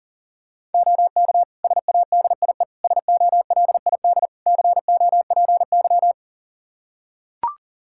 Morse code, heard as a keyed tone, about 40 wpm, 690 Hz. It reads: OK SADIE SOLID COPY
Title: Practice QSO - Exchange 3 - More Info - 40wpm